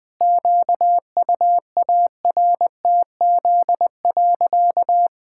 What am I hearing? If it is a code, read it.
Morse code, heard as a keyed tone, about 20 wpm, 700 Hz. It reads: QUARTZ.
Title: Pangram Challenge v2 - 20wpm